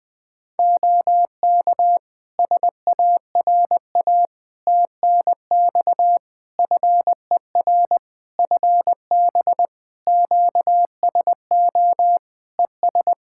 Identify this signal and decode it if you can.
Morse code, heard as a keyed tone, about 20 wpm, 700 Hz. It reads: OK SARA TNX FER FB QSO ES
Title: Practice QSO - Exchange 4 - Ending - 20wpm